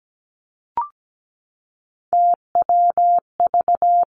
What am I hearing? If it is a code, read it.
Morse code, heard as a keyed tone, about 17 wpm, 700 Hz. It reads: TWV
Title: Three Challenging Letter-Number Combinations 17wpm